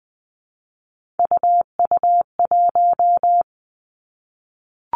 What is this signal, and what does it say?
Morse code, heard as a keyed tone, about 20 wpm, 700 Hz. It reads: UU1
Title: Three Challenging Letter-Number Combinations 20wpm